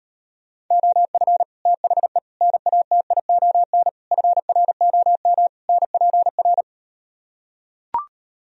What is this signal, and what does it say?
Morse code, heard as a keyed tone, about 38 wpm, 700 Hz. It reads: OF THE NATION FROM NPR